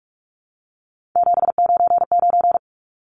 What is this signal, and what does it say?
Morse code, heard as a keyed tone, about 44 wpm, 700 Hz. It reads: 799
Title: Numbers - 3-Digit 45wpm